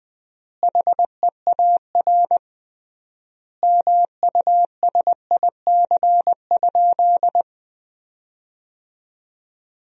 Morse code, heard as a keyed tone, about 20 wpm, 700 Hz. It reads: HEAR MUSIC?
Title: 3x Word-Spacing - Sentences from Top 300 Words - 20wpm